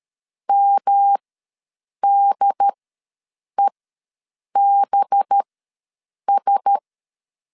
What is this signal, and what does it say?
Morse code, heard as a keyed tone, about 13 wpm, spaced as for 7 wpm, 800 Hz. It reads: MDEBS
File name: 5 LB ws 12wpm